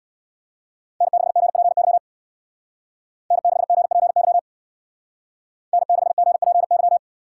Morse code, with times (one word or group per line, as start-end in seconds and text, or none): 1.0-2.0 N6GPY
3.3-4.4 N6GPY
5.7-7.0 N6GPY